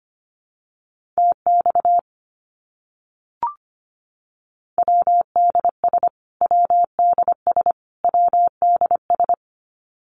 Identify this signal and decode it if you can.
Morse code, heard as a keyed tone, about 25 wpm, 700 Hz. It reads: TX WDS WDS WDS
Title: Abbreviations Repeated 3x - 25wpm